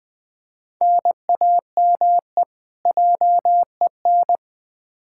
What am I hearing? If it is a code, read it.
Morse code, heard as a keyed tone, about 21 wpm, 700 Hz. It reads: NAME JEN
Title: K1USN SST QSO Practice 20wpm